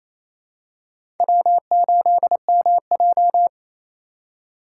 Morse code, heard as a keyed tone, about 28 wpm, 700 Hz. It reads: W8MJ
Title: USA Call Signs - 4 Characters - 2x Character Spacing - 28wpm